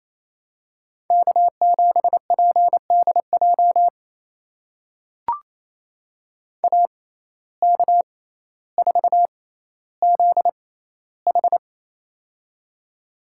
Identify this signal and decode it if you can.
Morse code, heard as a keyed tone, about 28 wpm, spaced as for 14 wpm, 700 Hz. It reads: K7PDJ AK4ZH
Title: USA Call Signs - 5 Characters - 6x Character Spacing - 28wpm